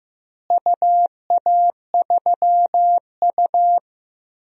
Morse code, heard as a keyed tone, about 15 wpm, 700 Hz. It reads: UA3U
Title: DXCC-100 Call Signs - 4 Characters - Repeated 2x - 15wpm